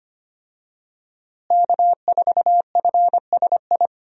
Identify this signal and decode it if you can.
Morse code, heard as a keyed tone, about 25 wpm, 700 Hz. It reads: K4FSI
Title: USA Call Signs - 5 Characters - Repeated 2x - 25wpm